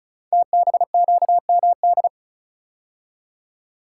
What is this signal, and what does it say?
Morse code, heard as a keyed tone, about 36 wpm, 700 Hz. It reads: TBQMD
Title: Code Groups - 5 Letters - 35wpm